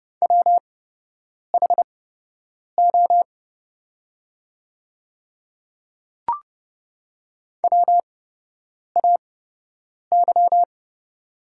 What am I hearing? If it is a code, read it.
Morse code, heard as a keyed tone, about 30 wpm, spaced as for 8 wpm, 700 Hz. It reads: WHO WAY
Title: 8x Character Spacing - Top 100 Words - 30wpm